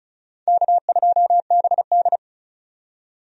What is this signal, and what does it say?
Morse code, heard as a keyed tone, about 36 wpm, 700 Hz. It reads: K2BD
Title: USA Call Signs - 4 Characters - 35wpm